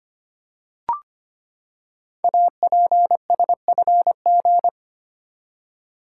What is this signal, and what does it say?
Morse code, heard as a keyed tone, about 25 wpm, 690 Hz. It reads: APSFG
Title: Code Groups - 5 Letters - 25wpm